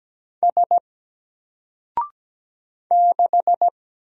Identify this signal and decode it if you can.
Morse code, heard as a keyed tone, about 17 wpm, 700 Hz. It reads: S 6
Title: Single Character (Letter:Number:Punctuation:Prosign) 17wpm